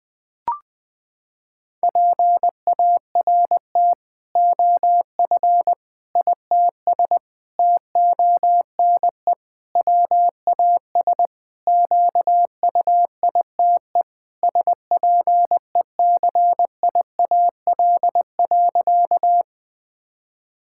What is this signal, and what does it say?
Morse code, heard as a keyed tone, about 21 wpm, 700 Hz. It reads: PART OF ITS TONE WAS QUITE SPECIAL.
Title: Sentences from Top 1600 Words 20wpm